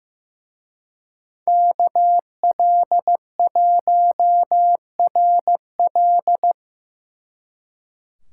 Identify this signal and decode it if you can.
Morse code, heard as a keyed tone, about 18 wpm, 700 Hz. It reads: KL1RL
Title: Speed Racing - USA Call Signs 15wpm